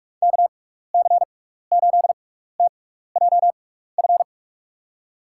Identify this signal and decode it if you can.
Morse code, heard as a keyed tone, about 42 wpm, spaced as for 15 wpm, 700 Hz. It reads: KC8TJF
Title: USA Call Signs - 6 Characters - 6x Character Spacing - 45wpm